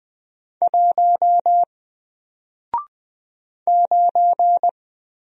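Morse code, heard as a keyed tone, about 20 wpm, 700 Hz. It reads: 1 9
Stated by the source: Single letter and number Morse drill at 20 wpm, from N4LCD.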